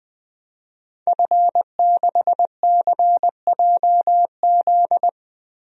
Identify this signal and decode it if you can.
Morse code, heard as a keyed tone, about 20 wpm, 700 Hz. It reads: F6CJZ
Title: DXCC-100 Call Signs - 5 Characters - Repeated 3x - 20wpm